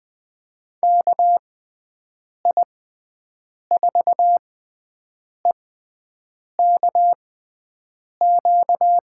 Morse code, heard as a keyed tone, about 20 wpm, spaced as for 7 wpm, 700 Hz. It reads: KI4EKQ
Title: USA Call Signs - 6 Characters - 6x Character Spacing - 20wpm